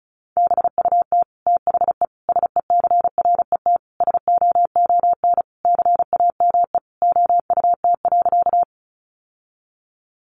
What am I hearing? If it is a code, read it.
Morse code, heard as a keyed tone, about 34 wpm, 700 Hz. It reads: BUT THE SECRET SOON CAME OUT.
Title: Sentences from Top 1200 Words 35wpm